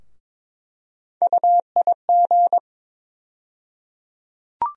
Morse code, heard as a keyed tone, about 22 wpm, 700 Hz. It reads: UIG
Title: USA Call Sign Postfixes - 3 Letters 22wpm